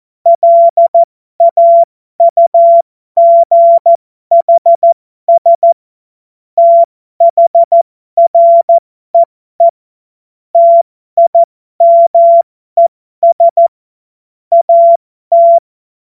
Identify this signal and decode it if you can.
Morse code, heard as a keyed tone, about 14 wpm, spaced as for 12 wpm, 680 Hz. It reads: LAUGHS THREE TIMES AT
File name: Morsekurs / Witze